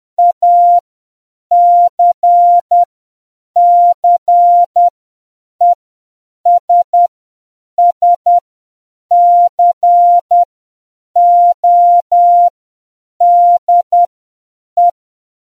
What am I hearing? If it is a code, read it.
Morse code, heard as a keyed tone, about 10 wpm, spaced as for 7 wpm, 700 Hz. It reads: ACCESS CODE